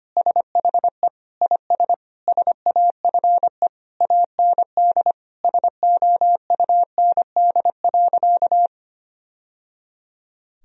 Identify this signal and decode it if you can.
Morse code, heard as a keyed tone, about 25 wpm, 700 Hz. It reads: SHE IS SAFE AND SOUND.
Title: Sentences from Top 1000 Words - Review Part 1 - 25wpm